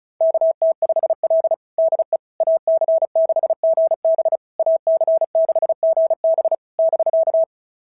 Morse code, heard as a keyed tone, about 36 wpm, 690 Hz. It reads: KT5L DE AC6GB AC6GB <BK>
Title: SOTA QSO Practice 35wpm